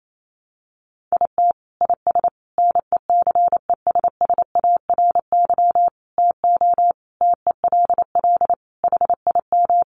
Morse code, heard as a keyed tone, about 28 wpm, 700 Hz. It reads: IT IS NECESSARY TO TELL HIM
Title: Sentences from Top 800 Words 28wpm